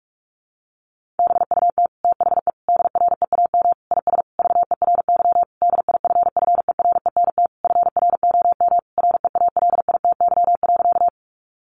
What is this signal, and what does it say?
Morse code, heard as a keyed tone, about 44 wpm, 700 Hz. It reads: BUT THE DREAM IS VERY DIFFERENT FROM REALITY.
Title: Sentences from Top 1500 Words 45wpm